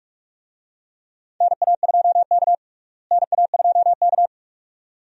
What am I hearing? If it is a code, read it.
Morse code, heard as a keyed tone, about 44 wpm, 700 Hz. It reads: NA2K NA2K
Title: USA Call Signs - 4 Characters - Repeated 2x - 45wpm